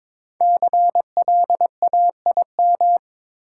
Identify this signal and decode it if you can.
Morse code, heard as a keyed tone, about 22 wpm, 700 Hz. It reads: CLAIM
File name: 5 Letter Words - First 3 Letters Given - 22wpm